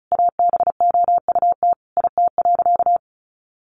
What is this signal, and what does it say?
Morse code, heard as a keyed tone, about 34 wpm, 700 Hz. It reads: ABOUT IT.
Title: Sentences from Top 1200 Words 35wpm